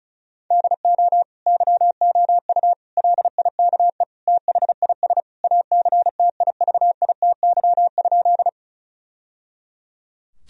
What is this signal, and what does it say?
Morse code, heard as a keyed tone, about 36 wpm, 700 Hz. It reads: DO YOU LIKE THIS ACTIVITY?
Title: Sentences from Top 1000 Words - Review Part 4 - 35wpm